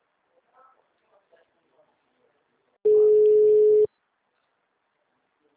Ringback tone (Europe, 425 Hz).
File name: out- -2014- - - .14196